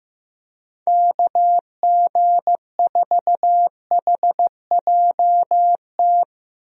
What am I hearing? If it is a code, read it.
Morse code, heard as a keyed tone, about 15 wpm, 700 Hz. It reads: KG4HJT
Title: USA Call Signs - 6 Characters - Repeated 3x - 15wpm